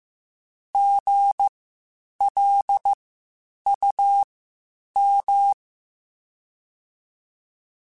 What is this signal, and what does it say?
Morse code, heard as a keyed tone, about 15 wpm, spaced as for 8 wpm, 800 Hz. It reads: GLUM
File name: Spaced Alphas / Spaced Alphas mp3s / Gs sp @13